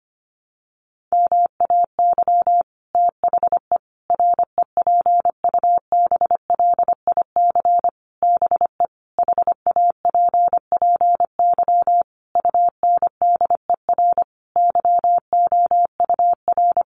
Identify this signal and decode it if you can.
Morse code, heard as a keyed tone, about 25 wpm, 700 Hz. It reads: MAY THE REPUBLIC BE HAPPY UNDER YOUR
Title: Sentences from Top 1500 Words 25wpm